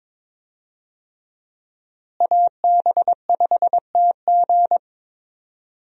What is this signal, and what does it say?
Morse code, heard as a keyed tone, about 22 wpm, 700 Hz. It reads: AB5TG
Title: USA Call Signs - 5 Characters - Repeated 3x - 22wpm